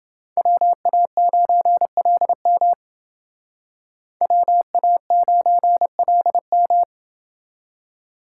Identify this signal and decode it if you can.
Morse code, heard as a keyed tone, about 29 wpm, 700 Hz. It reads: WA9LM WA9LM